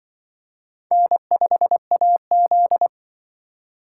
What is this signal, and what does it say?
Morse code, heard as a keyed tone, about 25 wpm, 700 Hz. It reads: N5AZ
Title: Speed Racing - USA 1x2 Call Signs 20wpm